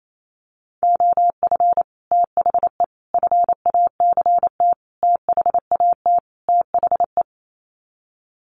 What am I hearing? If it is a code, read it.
Morse code, heard as a keyed tone, about 28 wpm, 700 Hz. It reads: OF THE FACT THAT THE
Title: Sets of 5 Words Encore2 28wpm